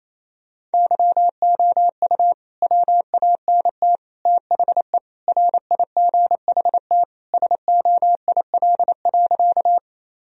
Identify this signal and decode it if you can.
Morse code, heard as a keyed tone, about 27 wpm, 700 Hz. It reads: YOU WANT THE RIGHT SOIL.
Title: Sentences from Top 1000 Words - Review Part 4 - 28wpm